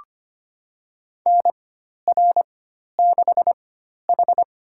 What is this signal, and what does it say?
Morse code, heard as a keyed tone, about 25 wpm, spaced as for 12 wpm, 700 Hz. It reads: NR6H
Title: USA Call Signs - 4 Characters - 4x Character Spacing - 25wpm